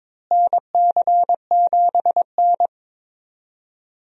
Morse code, heard as a keyed tone, about 22 wpm, 700 Hz. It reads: NC7N
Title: USA Call Signs - 4 Characters - Repeated 3x - 22wpm